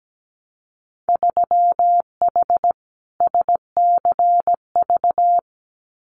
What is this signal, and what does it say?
Morse code, heard as a keyed tone, about 17 wpm, 700 Hz. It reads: 3H SCV
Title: Winter Field Day Exchanges Encore 17wpm